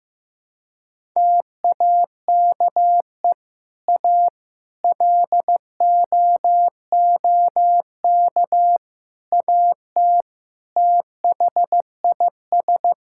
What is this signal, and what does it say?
Morse code, heard as a keyed tone, about 15 wpm, 700 Hz. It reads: TAKE A LOOK AT THIS
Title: Sentences from Top 1000 Words - Review Part 5 - 15wpm